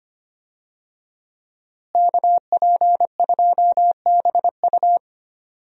Morse code, heard as a keyed tone, about 25 wpm, 700 Hz. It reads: KP2BU
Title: USA Call Signs - 5 Characters - Repeated 3x - 25wpm